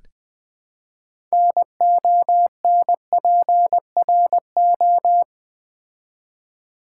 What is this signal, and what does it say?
Morse code, heard as a keyed tone, about 20 wpm, 700 Hz. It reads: NONPRO
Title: Build A Word - 3 Letters At A Time - 9-Letter Words - 20wpm